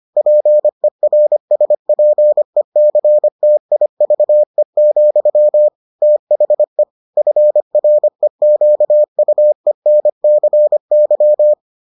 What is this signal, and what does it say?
Morse code, heard as a keyed tone, about 25 wpm, 600 Hz. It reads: PERSPECTIVE, THE FREQUENCY